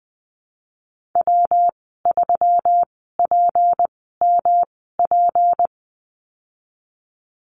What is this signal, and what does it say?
Morse code, heard as a keyed tone, about 20 wpm, spaced as for 15 wpm, 700 Hz. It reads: W3PMP
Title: 167 Lesson 27g - Number 7 - Call Signs 20wpm